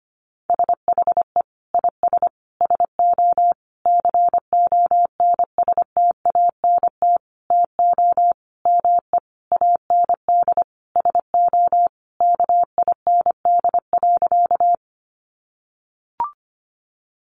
Morse code, heard as a keyed tone, about 25 wpm, 700 Hz. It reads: SHE IS SO CONSTANT TO ME AND SO KIND.
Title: Sentences from Top 1500 Words 25wpm